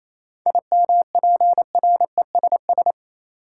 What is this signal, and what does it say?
Morse code, heard as a keyed tone, about 28 wpm, 700 Hz. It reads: IMPRESS
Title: Sentences from Top 1600 Words - New Word Drill - 28wpm